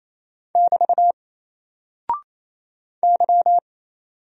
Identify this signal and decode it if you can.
Morse code, heard as a keyed tone, about 27 wpm, 700 Hz. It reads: <BT> Y
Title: Single Character (Letter:Number:Punctuation:Prosign) 28wpm